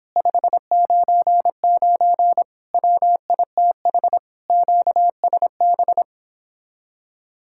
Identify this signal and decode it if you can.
Morse code, heard as a keyed tone, about 26 wpm, 700 Hz. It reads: 599 WITH QSB
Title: Speed-Racing Common QSO Phrases - 22wpm